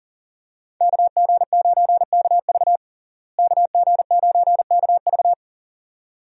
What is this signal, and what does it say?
Morse code, heard as a keyed tone, about 40 wpm, 700 Hz. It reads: KG9KV KG9KV
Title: USA Call Signs - 5 Characters - Repeated 2x - 40wpm